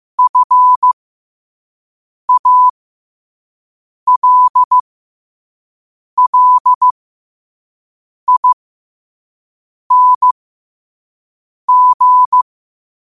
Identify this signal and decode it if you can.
Morse code, heard as a keyed tone, about 15 wpm, spaced as for 5 wpm, 1000 Hz. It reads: FALLING
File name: qotdfiles / 5wpm